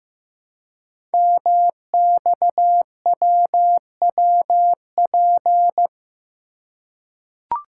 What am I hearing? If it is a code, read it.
Morse code, heard as a keyed tone, about 15 wpm, 700 Hz. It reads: MXWWP
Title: Code Groups - 5 Letters - 15wpm